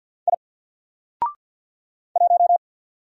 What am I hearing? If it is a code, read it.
Morse code, heard as a keyed tone, about 50 wpm, 700 Hz. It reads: I 1
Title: Single Letter-Number Alt 50wpm